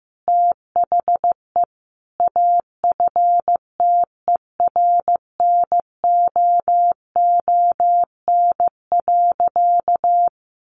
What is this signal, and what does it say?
Morse code, heard as a keyed tone, about 15 wpm, 700 Hz. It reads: THE AFTERNOON.
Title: Sentences from Top 1500 Words 15wpm